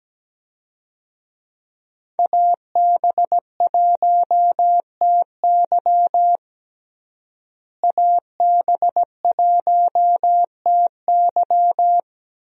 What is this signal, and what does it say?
Morse code, heard as a keyed tone, about 17 wpm, 700 Hz. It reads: AB1TY AB1TY
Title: USA Call Signs - 5 Characters - Repeated 2x - 17wpm